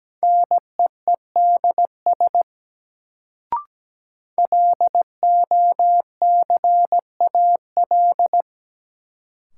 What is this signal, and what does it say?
Morse code, heard as a keyed tone, about 17 wpm, 700 Hz. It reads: NEEDS LOCAL
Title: Sentences from Top 1000 Words - Review Part 3 - New Word Drill - 17wpm